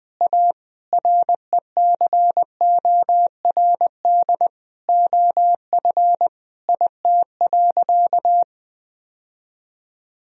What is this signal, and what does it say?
Morse code, heard as a keyed tone, about 20 wpm, 700 Hz. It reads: A RECORD OF IT.
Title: Sentences from Top 1000 Words - Review Part 3 - 20wpm